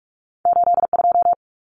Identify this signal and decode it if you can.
Morse code, heard as a keyed tone, about 46 wpm, 700 Hz. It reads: 82
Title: Common Cut Numbers - 2-Digit 45wpm